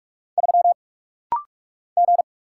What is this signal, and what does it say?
Morse code, heard as a keyed tone, about 47 wpm, 700 Hz. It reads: 3 G
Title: Single Letter-Number - Rapid-Fire 45wpm